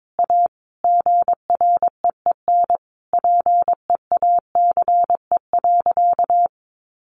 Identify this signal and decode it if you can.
Morse code, heard as a keyed tone, about 22 wpm, 700 Hz. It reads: A GREEN PEACE.
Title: A Sound Of Thunder 22wpm